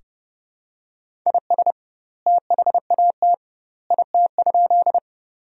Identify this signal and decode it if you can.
Morse code, heard as a keyed tone, about 30 wpm, 700 Hz. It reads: IS THAT IT?
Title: 2x Word-Spacing - Sentences from Top 100 Words - 30wpm